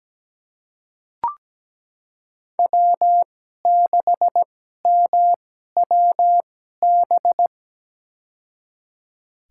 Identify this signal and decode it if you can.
Morse code, heard as a keyed tone, about 17 wpm, spaced as for 12 wpm, 700 Hz. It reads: W6MWB
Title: USA Call Signs - 5 Characters - 2x Character Spacing - 17wpm